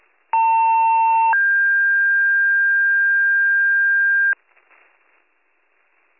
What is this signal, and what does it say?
Two-tone page, heard as a one-second tone, then a three-second tone, about 900 then 1600 Hz. It is heard from 0.3 to 4.3 s.